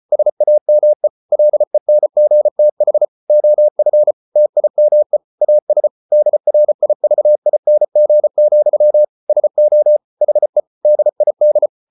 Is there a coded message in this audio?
Yes: Morse code from 0.1 to 11.7 s, heard as a keyed tone, about 34 wpm, 600 Hz. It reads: SAME LENGTH OF TIME AS DRIVING, SO HE DID